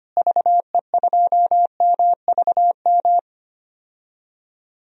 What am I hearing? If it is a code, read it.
Morse code, heard as a keyed tone, about 25 wpm, 700 Hz. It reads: VE2MVM
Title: DXCC-100 Call Signs - 6 Characters - Repeated 2x - 25wpm